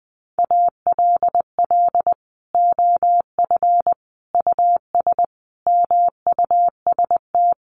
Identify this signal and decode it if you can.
Morse code, heard as a keyed tone, about 20 wpm, 700 Hz. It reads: ALL OF US MUST
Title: Sentences from Top 1400 Words 20wpm